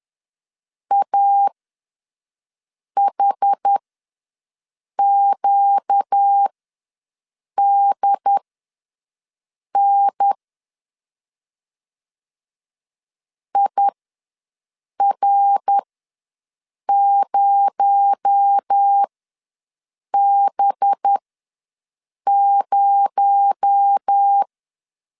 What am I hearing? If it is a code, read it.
Morse code, heard as a keyed tone, about 10 wpm, spaced as for 5 wpm, 800 Hz. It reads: AHQDN IR0B0
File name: LEZIONI / 6 PKX0 / 6 PKX0 ws 10wpm 4220